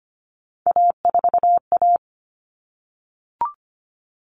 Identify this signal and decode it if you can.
Morse code, heard as a keyed tone, about 25 wpm, 700 Hz. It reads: A4A
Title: All Cut Numbers - 3-Digit 25wpm